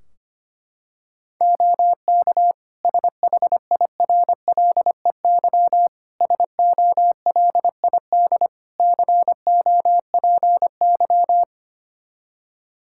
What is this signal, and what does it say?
Morse code, heard as a keyed tone, about 25 wpm, 700 Hz. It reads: OK SHIRLEY SOLID COPY
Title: Practice QSO - Exchange 3 - More Info - 25wpm